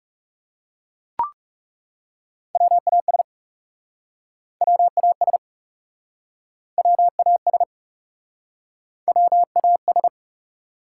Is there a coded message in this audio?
Yes: Morse code from 2.5 to 10.1 s, heard as a keyed tone, about 38 wpm, 700 Hz. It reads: WAS WAS WAS WAS